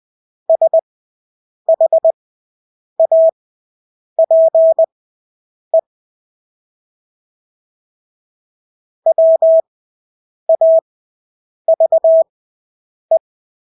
Morse code, heard as a keyed tone, about 20 wpm, spaced as for 8 wpm, 650 Hz. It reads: SHAPE WAVE
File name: Engelske ord 20/8 wpm